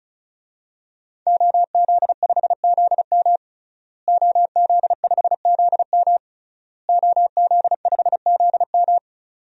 Morse code, heard as a keyed tone, about 36 wpm, 700 Hz. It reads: OZ5ZM OZ5ZM OZ5ZM
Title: DXCC-100 Call Signs - 5 Characters - Repeated 3x - 35wpm